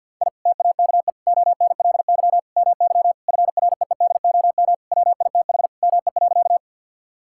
Morse code, heard as a keyed tone, about 50 wpm, 700 Hz. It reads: I TAKE ONLY MY FREEDOM WITH ME.
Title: Sentences from Top 1000 Words - Review Part 5 - 50wpm